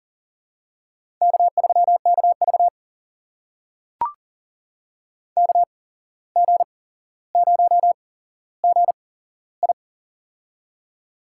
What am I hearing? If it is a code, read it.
Morse code, heard as a keyed tone, about 40 wpm, spaced as for 16 wpm, 700 Hz. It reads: K3KV KG0GI
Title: USA Call Signs - 8x Character Spacing - 40wpm